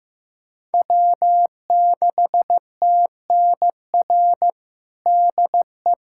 Morse code, heard as a keyed tone, about 16 wpm, 700 Hz. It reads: W6TNR DE
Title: USA Callsign DE Callsign - 5 Characters - 15wpm